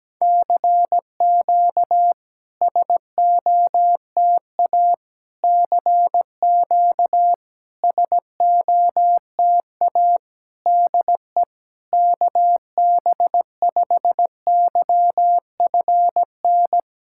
Morse code, heard as a keyed tone, about 17 wpm, 700 Hz. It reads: CQ SOTA CQ SOTA DE KB5YFN
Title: SOTA QSO Practice 17wpm